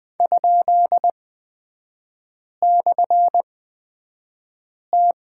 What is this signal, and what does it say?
Morse code, heard as a keyed tone, about 20 wpm, spaced as for 5 wpm, 700 Hz. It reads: ?/T
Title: 224 Lesson 36e - BT - Three Characters 20wpm